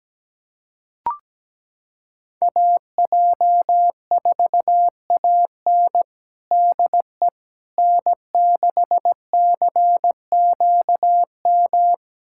Morse code, heard as a keyed tone, about 17 wpm, 700 Hz. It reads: AJ4AN DE N6CQM
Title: USA Callsign DE Callsign - 5 Characters - 17wpm